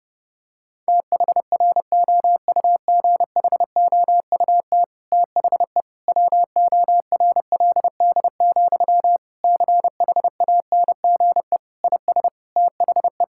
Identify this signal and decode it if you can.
Morse code, heard as a keyed tone, about 30 wpm, 700 Hz. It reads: THROUGHOUT THE WORLD, CHANGE IS THE
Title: Sentences from Top 1000 Words - Review Part 5 - 30wpm